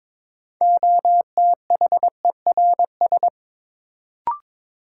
Morse code, heard as a keyed tone, about 22 wpm, 700 Hz. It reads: OTHERS
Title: Sentences from Top 1000 Words - Review Part 2 - New Word Drill - 22wpm